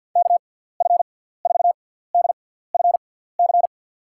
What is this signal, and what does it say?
Morse code, heard as a keyed tone, about 47 wpm, spaced as for 17 wpm, 700 Hz. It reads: KF4DFC
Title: USA Call Signs - 6 Characters - 6x Character Spacing - 50wpm